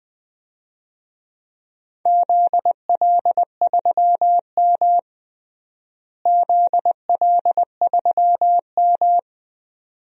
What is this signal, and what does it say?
Morse code, heard as a keyed tone, about 21 wpm, 700 Hz. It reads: ZL3M ZL3M
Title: DXCC-100 Call Signs - 4 Characters - Repeated 2x - 20wpm